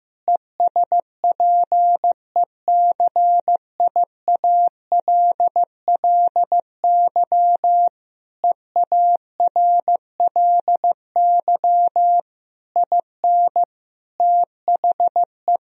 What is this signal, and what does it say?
Morse code, heard as a keyed tone, about 15 wpm, 700 Hz. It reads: ESPECIALLY EARLY IN THE
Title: Sentences from Top 1000 Words - Review Part 4 - 15wpm